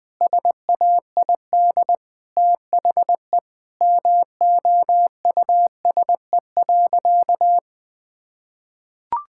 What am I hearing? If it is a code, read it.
Morse code, heard as a keyed tone, about 20 wpm, 700 Hz. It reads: SAID THE MOUSE.
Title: Sentences from Top 1000 Words - Review Part 5 - 20wpm